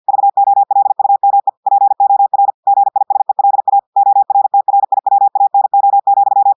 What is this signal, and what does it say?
Morse code, heard as a keyed tone, about 50 wpm, 800 Hz. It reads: VOLUME POR GISELA ORTRIWANO,
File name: audios / portugais / portugais texte en clair 1 vitesse 50 mots minutes